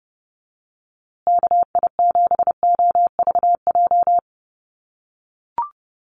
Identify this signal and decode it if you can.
Morse code, heard as a keyed tone, about 29 wpm, 700 Hz. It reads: KI7OVJ
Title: USA Call Signs Repeated 3x 30wpm